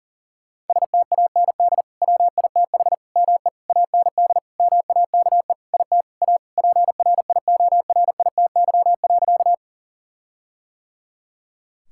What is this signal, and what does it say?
Morse code, heard as a keyed tone, about 40 wpm, 700 Hz. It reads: STAND WITH ME AND MAKE IT A PRIORITY.
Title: Sentences from Top 1600 Words 40wpm